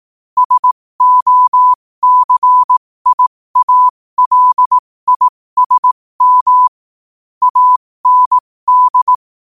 Morse code, heard as a keyed tone, about 18 wpm, spaced as for 15 wpm, 1000 Hz. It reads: SOCIALISM AND
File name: qotdfiles / 15wpm